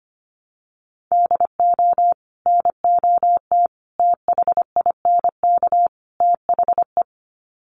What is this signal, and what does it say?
Morse code, heard as a keyed tone, about 25 wpm, 700 Hz. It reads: DO NOT THINK THE
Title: Sets of 4 Words 25wpm